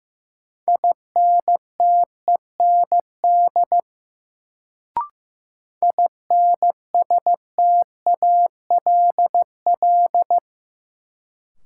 Morse code, heard as a keyed tone, about 15 wpm, 700 Hz. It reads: INTEND INSTALL
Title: Sentences from Top 1600 Words - New Word Drill - 15wpm